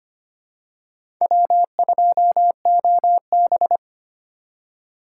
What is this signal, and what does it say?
Morse code, heard as a keyed tone, about 25 wpm, 700 Hz. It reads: W2OB
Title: USA Call Signs - 4 Characters - 4x Character Spacing - 25wpm